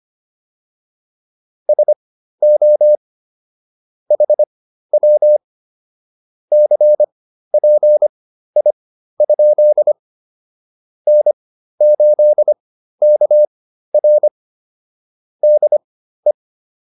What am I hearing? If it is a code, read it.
Morse code, heard as a keyed tone, about 25 wpm, spaced as for 13 wpm, 600 Hz. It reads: SO HW CPI? N8KR DE